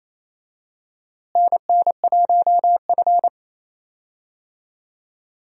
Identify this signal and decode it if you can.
Morse code, heard as a keyed tone, about 28 wpm, 700 Hz. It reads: NN1F
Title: USA Call Signs - 4 Characters - Repeated 2x - 28wpm